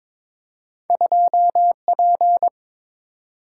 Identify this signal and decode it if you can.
Morse code, heard as a keyed tone, about 22 wpm, 700 Hz. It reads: 2P
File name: practice / 22wpm / Two Letter-Number Combinations - 22wpm